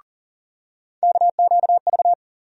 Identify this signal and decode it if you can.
Morse code, heard as a keyed tone, about 40 wpm, 700 Hz. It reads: KQV
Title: USA Call Sign Postfixes - 3 Letters 40wpm